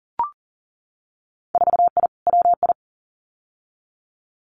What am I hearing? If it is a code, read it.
Morse code, heard as a keyed tone, about 37 wpm, 700 Hz. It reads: 4I WI
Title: Winter Field Day Exchanges 40wpm